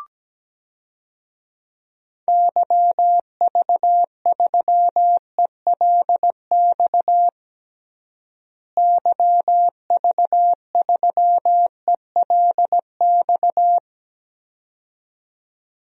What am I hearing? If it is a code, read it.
Morse code, heard as a keyed tone, about 17 wpm, 700 Hz. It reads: YV3ELX YV3ELX
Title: DXCC-100 Call Signs - 6 Characters - Repeated 2x - 17wpm